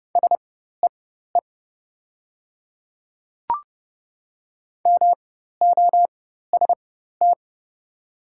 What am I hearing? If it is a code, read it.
Morse code, heard as a keyed tone, about 29 wpm, spaced as for 14 wpm, 700 Hz. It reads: SEE MOST
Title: 4x Character Spacing - Top 100 Words - 30wpm